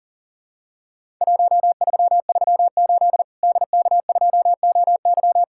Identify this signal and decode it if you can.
Morse code, heard as a keyed tone, about 38 wpm, 700 Hz. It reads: 1338 DK2OY
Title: Worked All Europe DX Contest - QTC Exchange - 40wpm